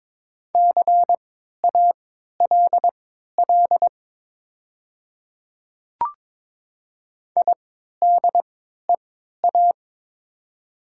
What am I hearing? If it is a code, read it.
Morse code, heard as a keyed tone, about 22 wpm, spaced as for 13 wpm, 700 Hz. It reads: CALL IDEA
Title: 3x Character Spacing - Top 300 Words - 22wpm